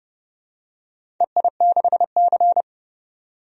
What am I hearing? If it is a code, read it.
Morse code, heard as a keyed tone, about 31 wpm, 700 Hz. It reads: EI6C
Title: DXCC-100 Call Signs - 4 Characters - 30wpm